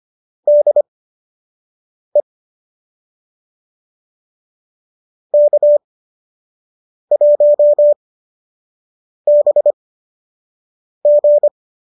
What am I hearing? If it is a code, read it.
Morse code, heard as a keyed tone, about 25 wpm, spaced as for 6 wpm, 590 Hz. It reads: DE K1BG